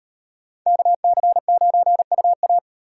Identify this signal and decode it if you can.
Morse code, heard as a keyed tone, about 36 wpm, 700 Hz. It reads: KC9UA